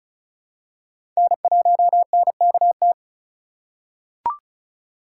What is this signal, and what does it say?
Morse code, heard as a keyed tone, about 34 wpm, 700 Hz. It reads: N1NKT
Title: USA Call Signs - 5 Characters - 3x Character Spacing - 35wpm